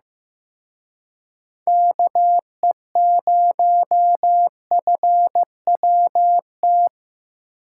Morse code, heard as a keyed tone, about 18 wpm, 700 Hz. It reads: KE0FWT
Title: Speed Racing - USA 2x3 Call Signs 15wpm